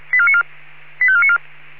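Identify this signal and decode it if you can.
Five-tone selective call (ZVEI), heard as quick runs of five tones, tones of about 70 ms. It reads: 85494
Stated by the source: ZVEI five-tone selective call received by SDR.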